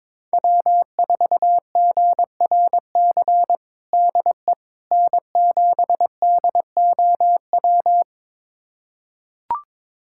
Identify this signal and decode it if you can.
Morse code, heard as a keyed tone, about 22 wpm, 700 Hz. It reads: W4GRC DE N7DOW
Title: USA Callsign DE Callsign - 5 Characters - 22wpm